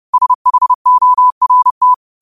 Morse code, heard as a keyed tone, about 30 wpm, 1000 Hz. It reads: SHORT